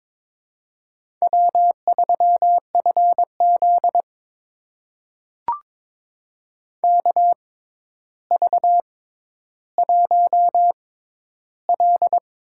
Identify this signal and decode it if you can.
Morse code, heard as a keyed tone, about 22 wpm, spaced as for 11 wpm, 700 Hz. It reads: W3FZ KV1L
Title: USA Call Signs - 4 Characters - 6x Character Spacing - 22wpm